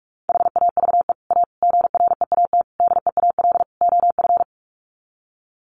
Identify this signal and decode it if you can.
Morse code, heard as a keyed tone, about 44 wpm, 700 Hz. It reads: HAVE A GREAT DEAL OF
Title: Sets of 5 Words Encore2 45wpm